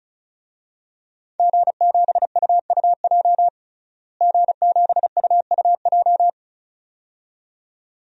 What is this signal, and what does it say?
Morse code, heard as a keyed tone, about 34 wpm, 700 Hz. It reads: G7UUJ G7UUJ
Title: DXCC-100 Call Signs - 5 Characters - Repeated 2x - 35wpm